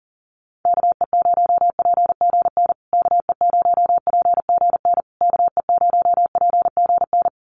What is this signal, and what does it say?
Morse code, heard as a keyed tone, about 40 wpm, 700 Hz. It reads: KE0PGN KE0PGN KE0PGN
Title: USA Call Signs Repeated 3x 40wpm